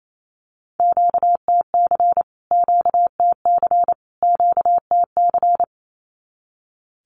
Morse code, heard as a keyed tone, about 28 wpm, 700 Hz. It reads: QTC QTC QTC
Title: Q-Codes Repeated 3x 28wpm